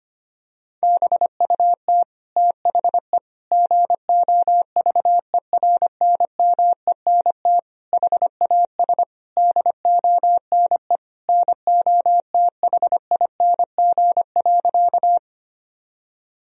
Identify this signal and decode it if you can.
Morse code, heard as a keyed tone, about 25 wpm, 700 Hz. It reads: BUT THE GOVERNMENT HAS DONE NOTHING.
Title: Sentences from Top 1000 Words - Review Part 3 - 25wpm